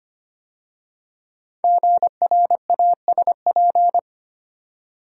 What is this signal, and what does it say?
Morse code, heard as a keyed tone, about 25 wpm, 700 Hz. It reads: GRASP
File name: 5 Letter Words - First 3 Letters Given - 25wpm